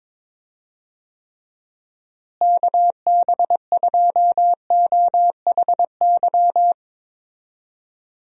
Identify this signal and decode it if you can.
Morse code, heard as a keyed tone, about 22 wpm, 700 Hz. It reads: KB2OHY
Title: USA Call Signs - 6 Characters - Repeated 3x - 22wpm